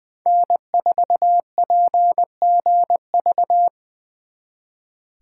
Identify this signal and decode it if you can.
Morse code, heard as a keyed tone, about 24 wpm, 700 Hz. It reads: N4PGV
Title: Speed Racing - USA 1x3 Call Signs 20wpm